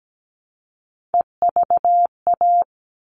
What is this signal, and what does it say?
Morse code, heard as a keyed tone, about 17 wpm, 700 Hz. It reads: EVA
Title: All Cut Numbers - 3-Digit 17wpm